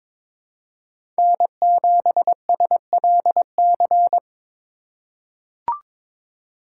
Morse code, heard as a keyed tone, about 23 wpm, 700 Hz. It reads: N7SLC
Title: USA Call Signs - 5 Characters - 3x Character Spacing - 22wpm